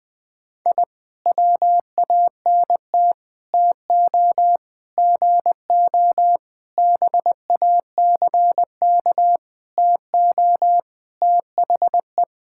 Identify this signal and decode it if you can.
Morse code, heard as a keyed tone, about 20 wpm, 700 Hz. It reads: I WANT TO GO BACK TO THE
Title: Sentences with 5-Word Phrases - Spoken Beginning - 20wpm